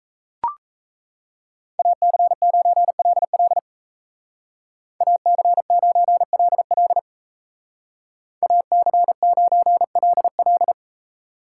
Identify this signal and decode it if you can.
Morse code, heard as a keyed tone, about 33 wpm, 700 Hz. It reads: AC9LL AC9LL AC9LL
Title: Speed Racing - USA 2x2 Call Signs 28wpm